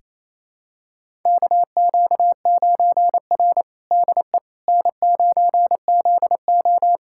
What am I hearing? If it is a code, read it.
Morse code, heard as a keyed tone, about 28 wpm, 700 Hz. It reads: KQ9R DE N9ZO
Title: USA Callsign DE Callsign - 4 Characters - 28wpm